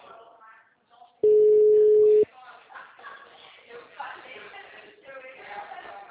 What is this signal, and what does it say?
Ringback tone (Europe, 425 Hz), with speech.